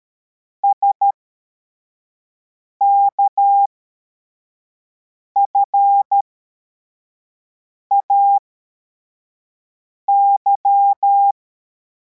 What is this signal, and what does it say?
Morse code, heard as a keyed tone, about 13 wpm, spaced as for 4 wpm, 800 Hz. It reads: SKFAY
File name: formation / cw / Text73